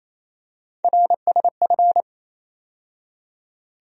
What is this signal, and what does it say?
Morse code, heard as a keyed tone, about 28 wpm, 700 Hz. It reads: RSF
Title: USA Call Sign Postfixes - 3 Letters 28wpm